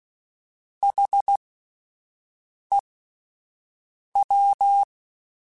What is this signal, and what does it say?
Morse code, heard as a keyed tone, about 16 wpm, spaced as for 5 wpm, 800 Hz. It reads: HEW
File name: Alphabetical / Alphabetical mp3s / Hs @5